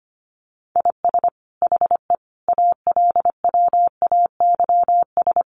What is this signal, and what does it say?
Morse code, heard as a keyed tone, about 25 wpm, 700 Hz. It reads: IS HE ALWAYS
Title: Sentences from Top 1400 Words 25wpm